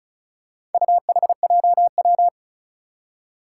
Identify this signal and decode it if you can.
Morse code, heard as a keyed tone, about 36 wpm, 700 Hz. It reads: UHJW
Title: Four Challenging Letter-Number Combinations 35wpm